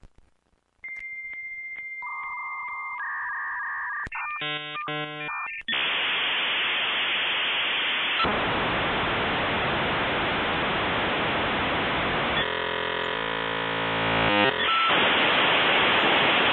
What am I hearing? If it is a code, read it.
Dial-up modem handshake.
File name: the-sound-of-dial-up-internet-6240